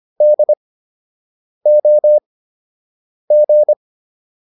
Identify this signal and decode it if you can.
Morse code, heard as a keyed tone, about 25 wpm, spaced as for 7 wpm, 600 Hz. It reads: DOG